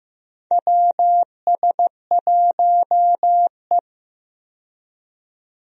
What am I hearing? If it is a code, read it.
Morse code, heard as a keyed tone, about 15 wpm, 700 Hz. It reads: WS1E
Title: USA Call Signs - 4 Characters - 15wpm